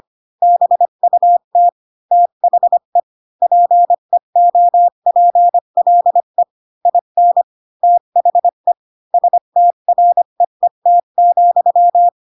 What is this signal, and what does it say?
Morse code, heard as a keyed tone, about 25 wpm, 700 Hz. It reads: BUT THE PEOPLE IN THE STREET,
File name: Pinocchio / Pinocchio ch3 20 25 ik1hge